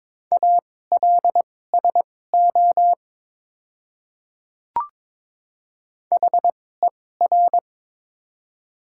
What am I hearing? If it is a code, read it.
Morse code, heard as a keyed tone, about 22 wpm, spaced as for 16 wpm, 700 Hz. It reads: ALSO HER
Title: 2x Character Spacing - Top 200 Words - 22wpm